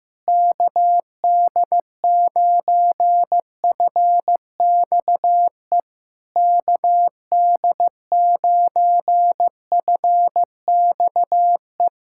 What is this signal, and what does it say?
Morse code, heard as a keyed tone, about 15 wpm, 700 Hz. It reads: KD9FXE KD9FXE
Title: SOTA QSO Practice 15wpm